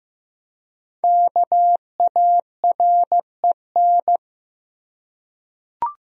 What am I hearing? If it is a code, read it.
Morse code, heard as a keyed tone, about 16 wpm, 700 Hz. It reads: KAREN
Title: USA Common Names - Max 5 Letters - 15wpm